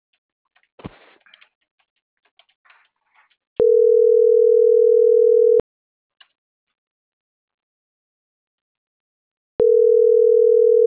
Ringback tone (North America), 440 + 480 Hz.